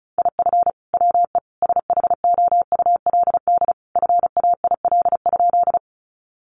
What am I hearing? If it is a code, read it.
Morse code, heard as a keyed tone, about 34 wpm, 700 Hz. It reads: IF WE SHOULD FAIL?
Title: Sentences from Top 1100 Words 35wpm